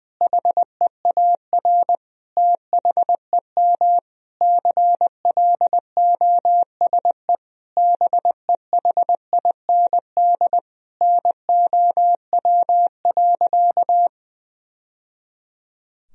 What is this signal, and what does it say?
Morse code, heard as a keyed tone, about 20 wpm, 700 Hz. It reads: HEAR THEM CLOSE BEHIND NOW.
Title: Sentences from Top 1000 Words - Review Part 1 - 20wpm